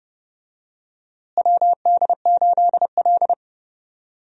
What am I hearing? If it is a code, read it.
Morse code, heard as a keyed tone, about 31 wpm, 700 Hz. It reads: WD8L
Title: USA Call Signs - 4 Characters - Repeated 2x - 30wpm